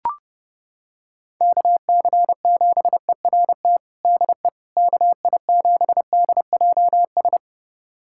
Morse code, heard as a keyed tone, about 30 wpm, 700 Hz. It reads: KC7ERT DE KI7DJS
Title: USA Callsign DE Callsign - 6 Characters - 30wpm